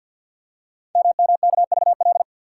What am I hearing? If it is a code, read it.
Morse code, heard as a keyed tone, about 52 wpm, 700 Hz. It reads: MMKVL
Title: Code Groups - 5 Letters - 50wpm